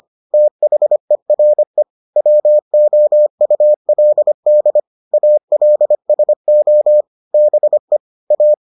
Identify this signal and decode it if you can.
Morse code, heard as a keyed tone, about 25 wpm, 600 Hz. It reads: THERE WOULD ALSO BE A